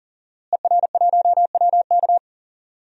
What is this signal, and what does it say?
Morse code, heard as a keyed tone, about 40 wpm, 700 Hz. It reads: ER1WK
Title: DXCC-100 Call Signs - 5 Characters - Repeated 3x - 40wpm